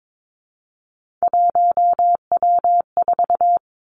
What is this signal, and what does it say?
Morse code, heard as a keyed tone, about 22 wpm, 700 Hz. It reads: 1W4
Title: Three Challenging Letter-Number Combinations 22wpm